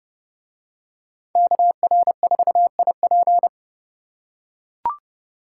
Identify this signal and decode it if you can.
Morse code, heard as a keyed tone, about 28 wpm, 700 Hz. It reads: KR4IP